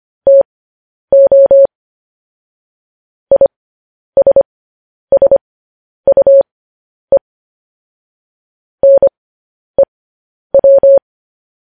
Morse code, heard as a keyed tone, about 25 wpm, spaced as for 10 wpm, 550 Hz. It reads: TO ISSUE NEW
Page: news headlines in morse code at 10 wpm